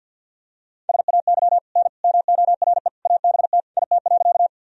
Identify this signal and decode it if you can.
Morse code, heard as a keyed tone, about 47 wpm, 700 Hz. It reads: SAY N MORE ABT IT.
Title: Sentences from Top 200 Words and Abbreviations 50wpm